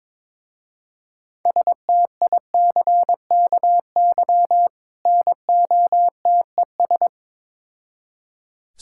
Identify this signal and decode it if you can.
Morse code, heard as a keyed tone, about 22 wpm, 700 Hz. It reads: STICKY NOTES
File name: Difficult Words with Hints - 22wpm